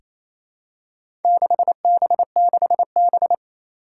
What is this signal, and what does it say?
Morse code, heard as a keyed tone, about 29 wpm, 700 Hz. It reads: 6B6B
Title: Four Challenging Letter-Number Combinations 28wpm